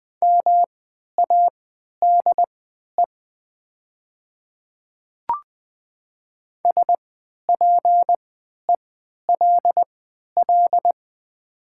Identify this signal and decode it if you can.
Morse code, heard as a keyed tone, about 20 wpm, spaced as for 11 wpm, 700 Hz. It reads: MADE SPELL
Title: 3x Character Spacing - Top 300 Words - 20wpm